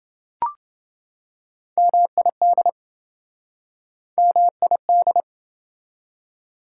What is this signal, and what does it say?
Morse code, heard as a keyed tone, about 27 wpm, 700 Hz. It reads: MID MID